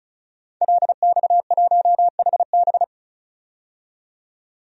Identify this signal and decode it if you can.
Morse code, heard as a keyed tone, about 34 wpm, 700 Hz. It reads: LX1HB
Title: DXCC-100 Call Signs - 5 Characters - 35wpm